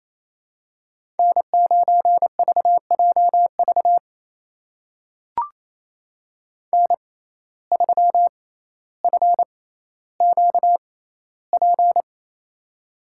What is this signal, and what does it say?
Morse code, heard as a keyed tone, about 28 wpm, spaced as for 14 wpm, 700 Hz. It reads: N9VJV N3FQP
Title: USA Call Signs - 5 Characters - 6x Character Spacing - 28wpm